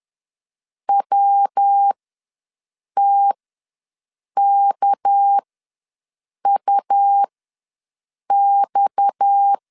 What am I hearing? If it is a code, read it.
Morse code, heard as a keyed tone, about 10 wpm, spaced as for 6 wpm, 800 Hz. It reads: WTKUX
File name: LEZIONI / 6 PKX0 / 6 PKX ws 10wpm 3706